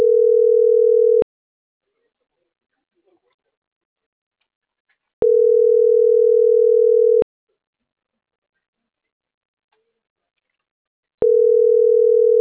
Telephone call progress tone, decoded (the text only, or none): ringback tone